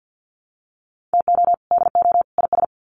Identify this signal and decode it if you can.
Morse code, heard as a keyed tone, about 48 wpm, 700 Hz. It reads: TO DO IS
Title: Sets of 3 Words 50wpm